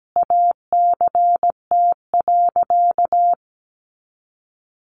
Morse code, heard as a keyed tone, about 17 wpm, 700 Hz. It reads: ACT.
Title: Sentences from Top 1100 Words 17wpm